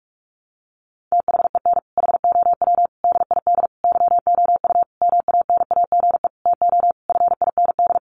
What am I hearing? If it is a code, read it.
Morse code, heard as a keyed tone, about 46 wpm, 700 Hz. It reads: THEN HOW DID YOU MANAGE TO FIND
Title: Sentences from Top 1400 Words 45wpm